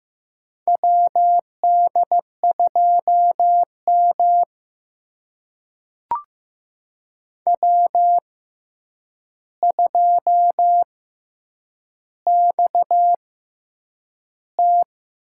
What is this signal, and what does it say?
Morse code, heard as a keyed tone, about 15 wpm, spaced as for 8 wpm, 700 Hz. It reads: WD2M W2XT
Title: USA Call Signs - 4 Characters - 6x Character Spacing - 15wpm